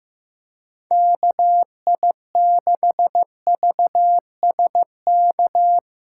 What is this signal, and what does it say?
Morse code, heard as a keyed tone, about 17 wpm, 700 Hz. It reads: KI6VSK
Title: Speed Racing - USA 2x3 Call Signs 15wpm